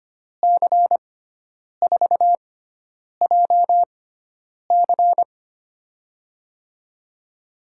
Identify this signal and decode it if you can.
Morse code, heard as a keyed tone, about 25 wpm, spaced as for 9 wpm, 700 Hz. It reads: C4JC
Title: USA Call Signs - 5 Characters - 6x Character Spacing - 25wpm